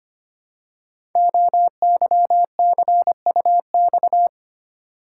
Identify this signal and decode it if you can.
Morse code, heard as a keyed tone, about 25 wpm, 700 Hz. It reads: OYCUX
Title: Code Groups - 5 Letter:Numbers - 25wpm